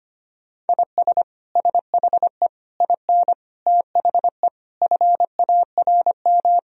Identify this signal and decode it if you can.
Morse code, heard as a keyed tone, about 25 wpm, 700 Hz. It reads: IS SHE IN THE FARM
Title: Sentences from Top 1000 Words - Review Part 2 - 25wpm